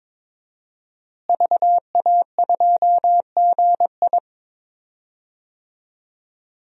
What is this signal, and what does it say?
Morse code, heard as a keyed tone, about 22 wpm, 700 Hz. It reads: VA2GI
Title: DXCC-100 Call Signs - 5 Characters - Repeated 2x - 22wpm